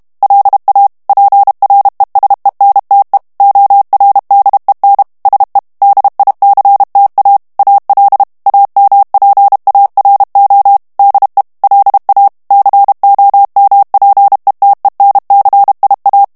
Morse code, heard as a keyed tone, about 32 wpm, 800 Hz. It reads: LA PRESENTE ORDEN SE DICTA AL AMPARO DE LA COMPETENCIA
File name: ESPAGNOL TEXTE EN CLAIR 2 VITESSE 30 MOTS MINUTES